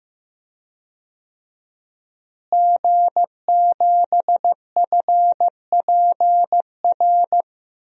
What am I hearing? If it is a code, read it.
Morse code, heard as a keyed tone, about 15 wpm, 700 Hz. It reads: G7FPR
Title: DXCC-100 Call Signs - 5 Characters - Repeated 2x - 15wpm